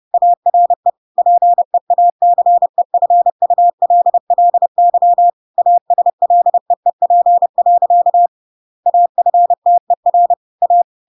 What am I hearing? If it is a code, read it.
Morse code, heard as a keyed tone, about 30 wpm, 700 Hz. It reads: ARE PEACEFULLY ASLEEP. AFTER A
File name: Pinocchio / Pinocchio ch6 30 30 ik1hge